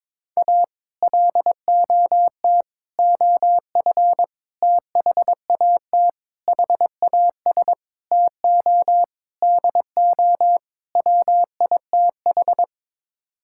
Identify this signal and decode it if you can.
Morse code, heard as a keyed tone, about 21 wpm, 700 Hz. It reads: A LOT OF THAT HAS TO DO WITH
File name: practice / 22wpm / Sets of 8 Words - 22wpm